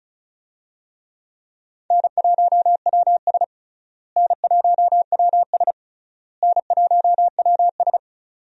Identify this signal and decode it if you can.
Morse code, heard as a keyed tone, about 36 wpm, 700 Hz. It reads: N1WS N1WS N1WS
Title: USA Call Signs - 4 Characters - Repeated 3x - 35wpm